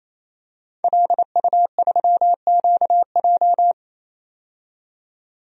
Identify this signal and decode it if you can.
Morse code, heard as a keyed tone, about 28 wpm, 700 Hz. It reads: LU3QJ
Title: DXCC-100 Call Signs - 5 Characters - 28wpm